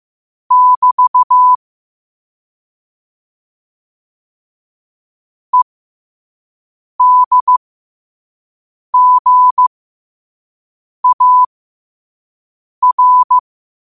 Morse code, heard as a keyed tone, about 15 wpm, spaced as for 5 wpm, 1000 Hz. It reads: <BT> EDGAR
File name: qotdfiles / 5wpm